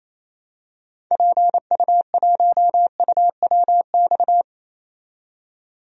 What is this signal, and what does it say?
Morse code, heard as a keyed tone, about 28 wpm, 700 Hz. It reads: PU1UWX
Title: DXCC-100 Call Signs - 6 Characters - 28wpm